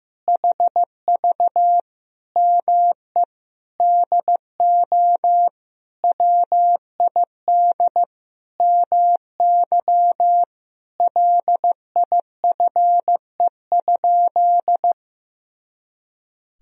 Morse code, heard as a keyed tone, about 15 wpm, 700 Hz. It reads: HV ME DO WID MY LIFE?
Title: Sentences from Top 300 Words and Abbreviations 15wpm